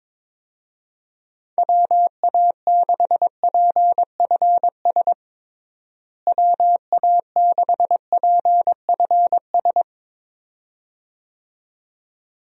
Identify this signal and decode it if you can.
Morse code, heard as a keyed tone, about 22 wpm, 700 Hz. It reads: WA6PFS WA6PFS
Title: USA Call Signs - 6 Characters - Repeated 2x - 22wpm